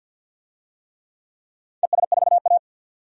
Morse code, heard as a keyed tone, about 51 wpm, 700 Hz. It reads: ES4A
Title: DXCC-100 Call Signs - 4 Characters - 50wpm